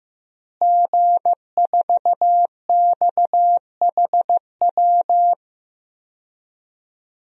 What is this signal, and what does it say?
Morse code, heard as a keyed tone, about 15 wpm, 700 Hz. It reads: G4XHW
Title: DXCC-100 Call Signs - 5 Characters - Repeated 2x - 15wpm